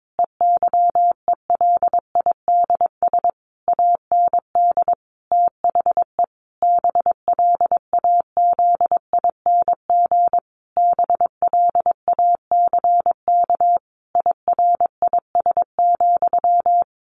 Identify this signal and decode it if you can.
Morse code, heard as a keyed tone, about 22 wpm, 700 Hz. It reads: EYELIDS AND THE BLAZING BLACK IRIS,
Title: A Sound Of Thunder 22wpm